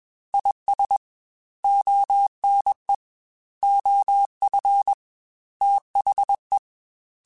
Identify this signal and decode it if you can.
Morse code, heard as a keyed tone, about 21 wpm, 800 Hz. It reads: IS ONE OF THE
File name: Quotes / Quotes-mp3s / ChildAsleep @20